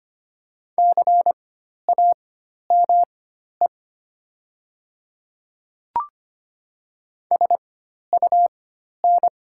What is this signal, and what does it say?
Morse code, heard as a keyed tone, about 25 wpm, spaced as for 12 wpm, 700 Hz. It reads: CAME SUN
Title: 4x Character Spacing - Top 300 Words - 25wpm